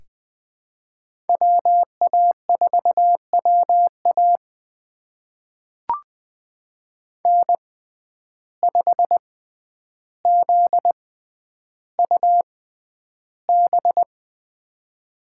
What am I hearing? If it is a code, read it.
Morse code, heard as a keyed tone, about 20 wpm, spaced as for 10 wpm, 700 Hz. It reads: WA4WA N5ZUB
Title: USA Call Signs - 6x Character Spacing - 20wpm